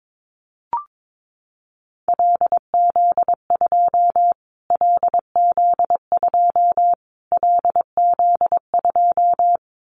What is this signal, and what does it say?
Morse code, heard as a keyed tone, about 22 wpm, 700 Hz. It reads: LZ2 LZ2 LZ2
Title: Callsign Prefixes for Top 25 DXCC - Repeated 3x - 22wpm